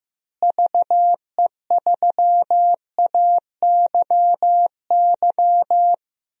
Morse code, heard as a keyed tone, about 15 wpm, 700 Hz. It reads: VE3AYY
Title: DXCC-100 Call Signs - 6 Characters - Repeated 3x - 15wpm